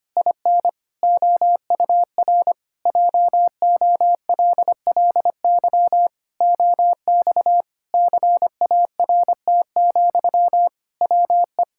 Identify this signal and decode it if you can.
Morse code, heard as a keyed tone, about 25 wpm, 700 Hz. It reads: IN OUR JOLLY OX CART, WE
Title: Pangram Challenge v2 - 25wpm